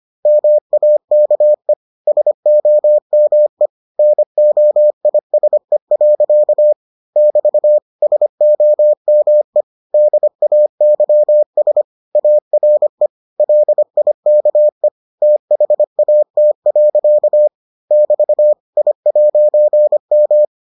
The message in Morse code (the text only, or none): MAKE SOME NOISE. <BT> SOME DAYS ARE LIKE THAT. <BT> I'M